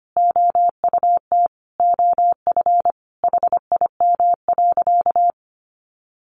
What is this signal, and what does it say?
Morse code, heard as a keyed tone, about 25 wpm, 700 Hz. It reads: OUT OF HIM.
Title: Sentences from Top 1500 Words 25wpm